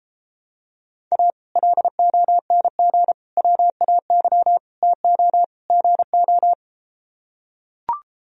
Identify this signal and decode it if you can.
Morse code, heard as a keyed tone, about 33 wpm, 700 Hz. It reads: A LONG WAY TO GO